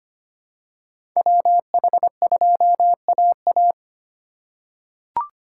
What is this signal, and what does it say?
Morse code, heard as a keyed tone, about 25 wpm, 700 Hz. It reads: WH2AA
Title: USA Call Signs - 5 Characters - 3x Character Spacing - 25wpm